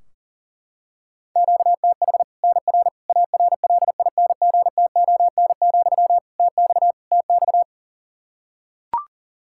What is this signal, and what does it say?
Morse code, heard as a keyed tone, about 40 wpm, 700 Hz. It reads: QTH NR ARLINGTON, TX TX
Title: Practice QSO - Exchange 1 - Beginning - 40wpm